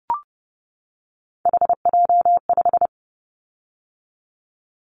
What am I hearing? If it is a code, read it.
Morse code, heard as a keyed tone, about 30 wpm, 700 Hz. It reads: HJ5
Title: Callsign Prefixes for Top 100 DXCC - 30wpm